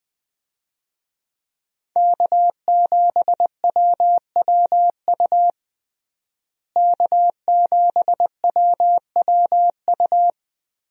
Morse code, heard as a keyed tone, about 20 wpm, 700 Hz. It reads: K7WWU K7WWU
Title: USA Call Signs - 5 Characters - Repeated 2x - 20wpm